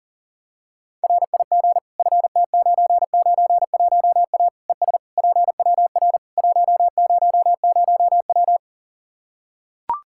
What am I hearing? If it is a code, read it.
Morse code, heard as a keyed tone, about 37 wpm, 700 Hz. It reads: RIG FT991A ES PWR 100W
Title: K1USN SST QSO Practice 40wpm